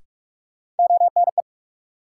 Morse code, heard as a keyed tone, about 46 wpm, 700 Hz. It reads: ONE
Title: Sets of 2 Words - New Word Drill - 45wpm